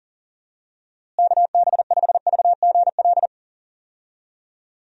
Morse code, heard as a keyed tone, about 40 wpm, 700 Hz. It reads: KB5VGL
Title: USA Call Signs - 6 Characters - Repeated 3x - 40wpm